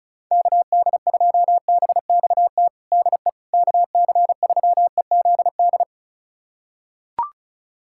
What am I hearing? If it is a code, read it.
Morse code, heard as a keyed tone, about 36 wpm, 700 Hz. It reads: KD2BXT DE KC3EZD
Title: USA Callsign DE Callsign 35wpm